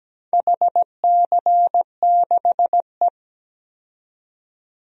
Morse code, heard as a keyed tone, about 17 wpm, 700 Hz. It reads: HC6E
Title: DXCC-100 Call Signs - 4 Characters - 17wpm